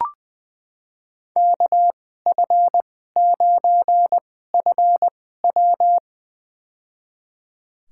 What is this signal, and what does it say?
Morse code, heard as a keyed tone, about 20 wpm, spaced as for 14 wpm, 700 Hz. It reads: KF9FW
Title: USA Call Signs - 5 Characters - 2x Character Spacing - 20wpm